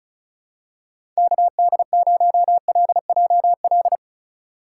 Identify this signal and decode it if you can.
Morse code, heard as a keyed tone, about 34 wpm, 700 Hz. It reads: KD0LJL
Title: USA Call Signs - 2x Character Spacing - 35wpm